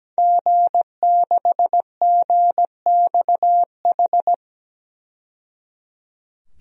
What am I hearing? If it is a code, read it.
Morse code, heard as a keyed tone, about 17 wpm, 700 Hz. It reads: G6GXH
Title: DXCC-100 Call Signs - 5 Characters - 17wpm